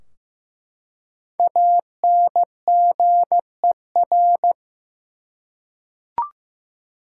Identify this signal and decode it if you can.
Morse code, heard as a keyed tone, about 15 wpm, 700 Hz. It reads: ANGER
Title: Five Letter Words - 15wpm